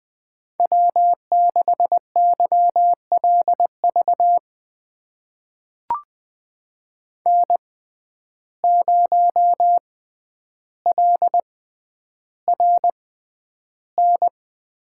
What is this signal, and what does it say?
Morse code, heard as a keyed tone, about 20 wpm, spaced as for 10 wpm, 700 Hz. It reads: W6YLV N0LRN
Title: USA Call Signs - 5 Characters - 6x Character Spacing - 20wpm